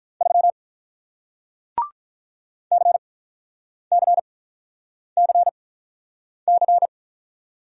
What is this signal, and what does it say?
Morse code, heard as a keyed tone, about 48 wpm, spaced as for 9 wpm, 700 Hz. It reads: 3 CCCC